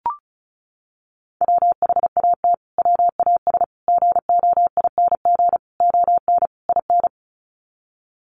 Morse code, heard as a keyed tone, about 34 wpm, 700 Hz. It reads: WHAT WAS GOING ON IN
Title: Sets of 5 Words Encore 35wpm